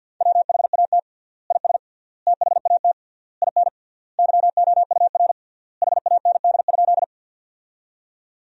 Morse code, heard as a keyed tone, about 50 wpm, 700 Hz. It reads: WHAT IS THAT IN YOUR HAND?
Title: 3x Word-Spacing - Sentences from Top 200 Words - 50wpm